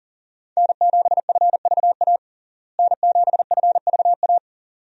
Morse code, heard as a keyed tone, about 40 wpm, 700 Hz. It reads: N7FVA N7FVA
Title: USA Call Signs - 5 Characters - Repeated 2x - 40wpm